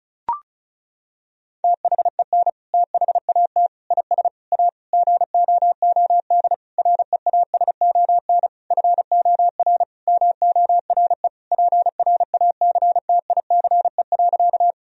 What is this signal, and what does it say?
Morse code, heard as a keyed tone, about 34 wpm, 700 Hz. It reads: THEN THAT IS A GOOD REASON FOR MORE PRACTICE.
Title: Sentences from Top 1000 Words - Review Part 4 - 35wpm